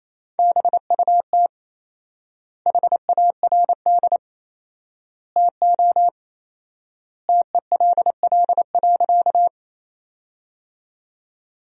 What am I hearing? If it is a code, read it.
Morse code, heard as a keyed tone, about 28 wpm, 700 Hz. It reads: BUT HARD TO TELL.
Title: 4x Word-Spacing - Sentences from Top 300 Words - 28wpm